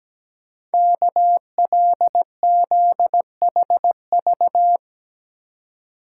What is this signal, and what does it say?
Morse code, heard as a keyed tone, about 17 wpm, 700 Hz. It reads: KLZHV
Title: Code Groups - 5 Letters - 17wpm